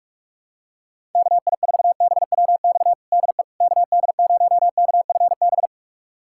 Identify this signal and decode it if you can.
Morse code, heard as a keyed tone, about 42 wpm, 700 Hz. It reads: KI4BWX DE KD0KFB
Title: USA Callsign DE Callsign - 6 Characters - 45wpm